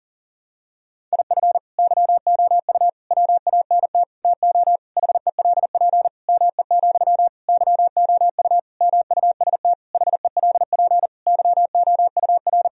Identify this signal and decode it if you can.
Morse code, heard as a keyed tone, about 37 wpm, 700 Hz. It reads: IF YOU WANT TO HELP ME, YOU MUST HELP YOUR
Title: Sentences from Top 1000 Words - Review Part 1 - 40wpm